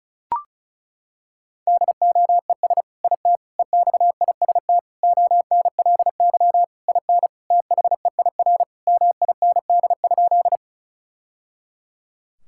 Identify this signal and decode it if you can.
Morse code, heard as a keyed tone, about 36 wpm, 700 Hz. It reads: DOES IT EXIST ONLY IN THEIR MIND?
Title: Sentences from Top 1000 Words - Review Part 5 - 35wpm